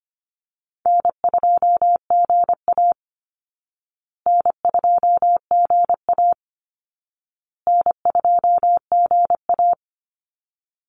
Morse code, heard as a keyed tone, about 25 wpm, 700 Hz. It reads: N2GA N2GA N2GA
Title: Easy Call Signs Repeated 3x - 25wpm